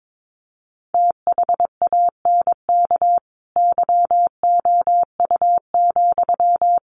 Morse code, heard as a keyed tone, about 22 wpm, 700 Hz. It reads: THANK YOU,
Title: Sentences from Top 1500 Words 22wpm